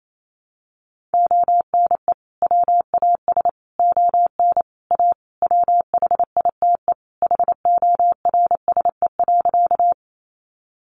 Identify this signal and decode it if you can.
Morse code, heard as a keyed tone, about 28 wpm, 700 Hz. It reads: ONE WAS ON A WHITE HORSE.
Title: Sentences from Top 300 Words 28wpm